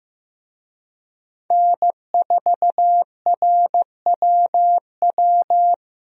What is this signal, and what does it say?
Morse code, heard as a keyed tone, about 16 wpm, 700 Hz. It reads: N4RWW
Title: USA Call Signs - 5 Characters - Repeated 2x - 15wpm